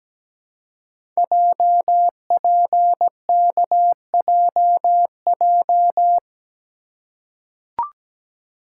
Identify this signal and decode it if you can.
Morse code, heard as a keyed tone, about 17 wpm, 700 Hz. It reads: JPKJJ
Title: Code Groups - 5 Letter:Numbers - 17wpm